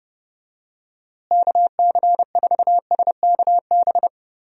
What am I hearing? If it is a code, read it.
Morse code, heard as a keyed tone, about 32 wpm, 700 Hz. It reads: KC4SKB
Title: USA Call Signs - 6 Characters - Repeated 2x - 30wpm